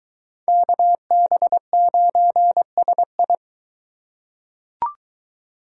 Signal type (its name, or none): Morse code